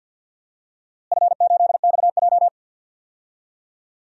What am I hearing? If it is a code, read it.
Morse code, heard as a keyed tone, about 50 wpm, 700 Hz. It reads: F8XJ
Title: DXCC-100 Call Signs - 4 Characters - Repeated 2x - 50wpm